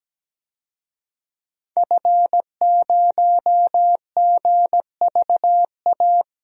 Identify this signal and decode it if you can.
Morse code, heard as a keyed tone, about 17 wpm, 700 Hz. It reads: F0GVA
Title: DXCC-100 Call Signs - 5 Characters - 17wpm